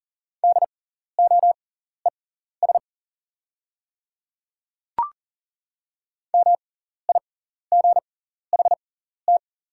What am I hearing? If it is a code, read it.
Morse code, heard as a keyed tone, about 40 wpm, spaced as for 14 wpm, 690 Hz. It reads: DOES MIGHT
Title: 6x Character Spacing - Top 300 Words - 40wpm